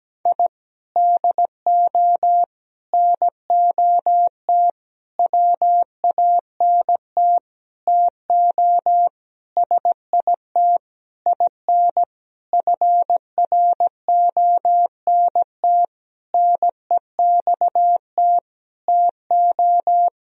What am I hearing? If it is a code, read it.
Morse code, heard as a keyed tone, about 17 wpm, 700 Hz. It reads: I DO NOT WANT TO SIT IN FRONT NEXT TO
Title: Sentences from Top 1000 Words - Review Part 2 - 17wpm